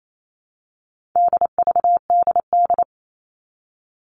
Morse code, heard as a keyed tone, about 28 wpm, 700 Hz. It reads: DVDD
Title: All Cut Numbers - 4-Digit 28wpm